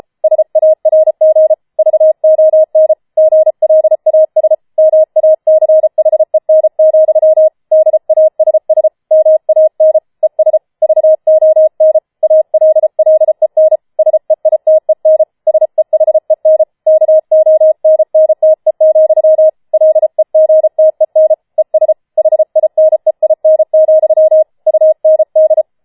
Morse code, heard as a keyed tone, about 33 wpm, 630 Hz. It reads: SARG VON GLAS MACHEN, DASS MAN ES VON ALLEN SEITEN SEHEN KONNTE, LEGTEN ES HINEIN, UND